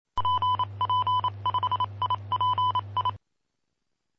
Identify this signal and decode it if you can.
Morse code, heard as a keyed tone, about 27 wpm, 1000 Hz. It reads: PP5IPI
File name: pp5bar / aulas de telegrafia / Aula-07